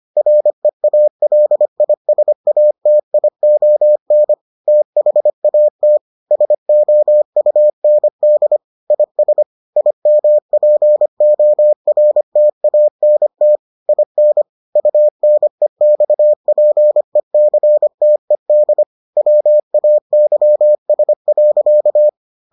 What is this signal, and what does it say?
Morse code, heard as a keyed tone, about 25 wpm, 600 Hz. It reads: REALISATION THAT SOUND IS IMPORTANT IN UNEXPECTED WAYS.